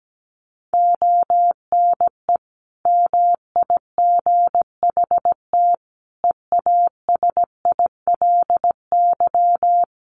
Morse code, heard as a keyed tone, about 17 wpm, 700 Hz. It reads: ONE MIGHT EASILY
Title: Sentences from Top 1400 Words 17wpm